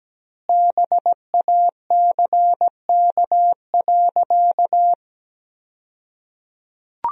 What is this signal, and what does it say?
Morse code, heard as a keyed tone, about 17 wpm, 700 Hz. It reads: BACK.
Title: 2x Word-Spacing - Sentences from Top 300 Words - 17wpm